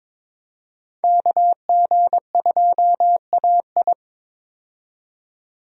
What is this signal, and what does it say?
Morse code, heard as a keyed tone, about 21 wpm, 700 Hz. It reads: KG2AI
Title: USA Call Signs - 5 Characters - 22wpm